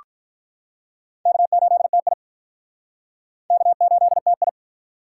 Morse code, heard as a keyed tone, about 45 wpm, 700 Hz. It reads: K8TI K8TI